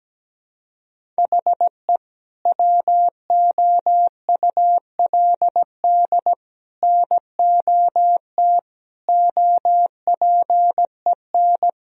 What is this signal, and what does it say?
Morse code, heard as a keyed tone, about 17 wpm, 700 Hz. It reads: HE WOULD NOT OPEN
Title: Sentences from Top 1000 Words - Review Part 1 - 17wpm